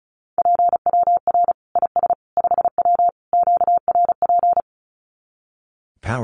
Morse code, heard as a keyed tone, about 34 wpm, 700 Hz. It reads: PWR IS 5W QRP
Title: Common QSO Phrases 35wpm